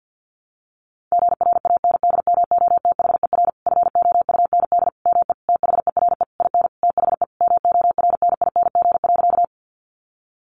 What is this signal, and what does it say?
Morse code, heard as a keyed tone, about 50 wpm, 700 Hz. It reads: GRANDMOTHER FOUND ME THERE IN THE MORNING.
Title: Sentences from Top 1200 Words 50wpm